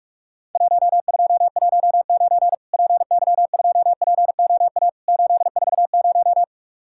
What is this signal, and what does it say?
Morse code, heard as a keyed tone, about 46 wpm, 700 Hz. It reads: 1219 PY2POA 840
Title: Worked All Europe DX Contest - QTC Exchange - 45wpm